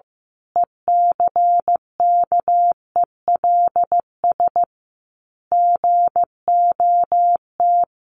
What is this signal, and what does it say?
Morse code, heard as a keyed tone, about 15 wpm, 700 Hz. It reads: ECKELS GOT
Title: A Sound Of Thunder 15wpm